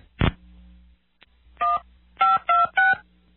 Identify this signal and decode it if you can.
DTMF, heard as telephone touch tones, tones of about 180 ms. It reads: 1 526